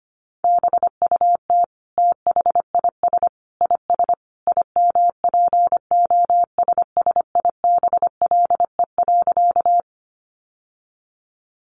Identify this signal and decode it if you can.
Morse code, heard as a keyed tone, about 25 wpm, 700 Hz. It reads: BUT THIS IS IMPOSSIBLE.
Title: Sentences from Top 1100 Words 25wpm